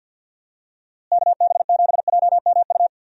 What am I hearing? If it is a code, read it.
Morse code, heard as a keyed tone, about 50 wpm, 700 Hz. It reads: KB7JMU
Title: USA Call Signs - 6 Characters - Repeated 3x - 50wpm